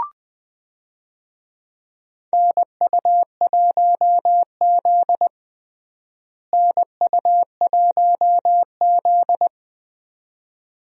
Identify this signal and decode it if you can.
Morse code, heard as a keyed tone, about 20 wpm, 700 Hz. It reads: NU1Z NU1Z
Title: USA Call Signs - 4 Characters - Repeated 2x - 20wpm